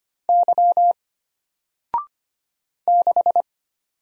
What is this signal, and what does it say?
Morse code, heard as a keyed tone, about 25 wpm, 700 Hz. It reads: Y 6
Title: Single Letter-Number Alt 25wpm